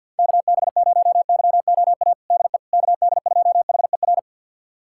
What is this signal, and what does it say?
Morse code, heard as a keyed tone, about 52 wpm, 700 Hz. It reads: KB0YOA DE KD2HER
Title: USA Callsign DE Callsign - 6 Characters - 50wpm